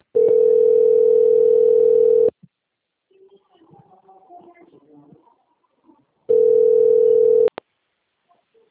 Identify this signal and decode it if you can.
Ringback tone (North America), 440 + 480 Hz.